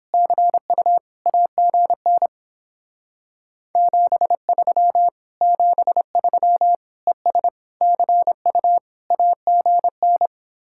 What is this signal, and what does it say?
Morse code, heard as a keyed tone, about 31 wpm, 700 Hz. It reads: CU AGN 73 73 ES CU AGN